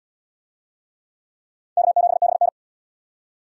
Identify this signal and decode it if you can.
Morse code, heard as a keyed tone, about 67 wpm, 700 Hz. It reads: K7BN